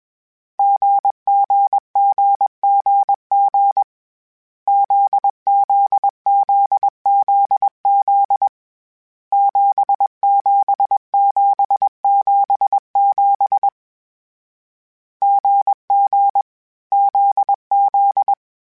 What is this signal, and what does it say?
Morse code, heard as a keyed tone, about 21 wpm, 800 Hz. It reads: GGGGG ZZZZZ 77777 GG ZZ